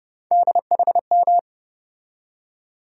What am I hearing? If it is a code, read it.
Morse code, heard as a keyed tone, about 30 wpm, 700 Hz. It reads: DHM
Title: USA Call Sign Postfixes - 3 Letters 30wpm